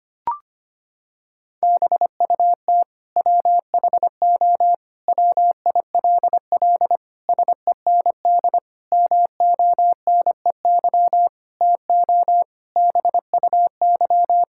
Morse code, heard as a keyed tone, about 25 wpm, 700 Hz. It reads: BUT WHO WILL SEND MONEY TO BUY
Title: Sentences from Top 1600 Words 25wpm